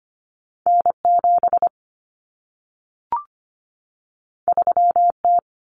Morse code, heard as a keyed tone, about 25 wpm, 700 Hz. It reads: N7 3T
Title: Common Cut Numbers - 2-Digit 25wpm